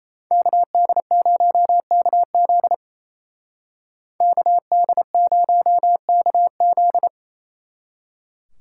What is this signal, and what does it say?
Morse code, heard as a keyed tone, about 36 wpm, 700 Hz. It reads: KD0KZ KD0KZ